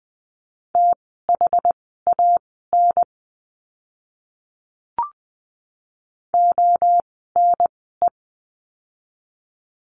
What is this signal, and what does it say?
Morse code, heard as a keyed tone, about 20 wpm, spaced as for 14 wpm, 700 Hz. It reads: THAN ONE
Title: 185 Lesson 30f - Letter b - Words 20wpm